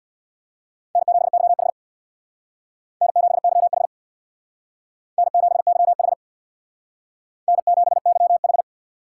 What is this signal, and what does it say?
Morse code, heard as a keyed tone, about 62 wpm, 700 Hz. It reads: N7YH N7YH N7YH N7YH
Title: Speed Racing - USA Call Signs 50wpm